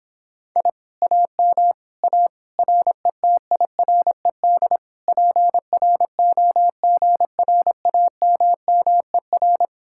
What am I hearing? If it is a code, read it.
Morse code, heard as a keyed tone, about 31 wpm, 700 Hz. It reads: I AM A RETIRED PROGRAMMER